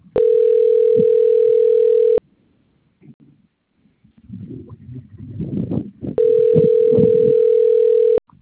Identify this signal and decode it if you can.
Ringback tone (North America), 440 + 480 Hz.